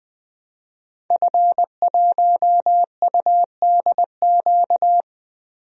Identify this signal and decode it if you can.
Morse code, heard as a keyed tone, about 20 wpm, 700 Hz. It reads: F1UDQ
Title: DXCC-100 Call Signs - 5 Characters - 20wpm